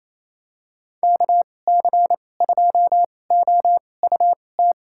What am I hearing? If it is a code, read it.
Morse code, heard as a keyed tone, about 28 wpm, spaced as for 20 wpm, 700 Hz. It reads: KC2OUT
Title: USA Call Signs - 6 Characters - 2x Character Spacing - 28wpm